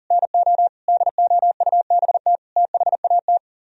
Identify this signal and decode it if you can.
Morse code, heard as a keyed tone, about 52 wpm, 700 Hz. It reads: NO DOUBT THAT